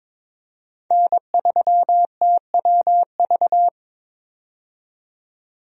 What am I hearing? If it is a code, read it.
Morse code, heard as a keyed tone, about 22 wpm, 700 Hz. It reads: N3TWV
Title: USA Call Signs - 5 Characters - 22wpm